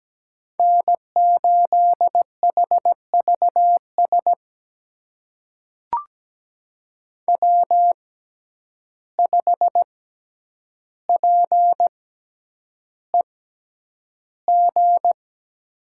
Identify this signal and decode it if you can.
Morse code, heard as a keyed tone, about 17 wpm, spaced as for 9 wpm, 700 Hz. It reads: N8HVS W5PEG
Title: USA Call Signs - 5 Characters - 6x Character Spacing - 17wpm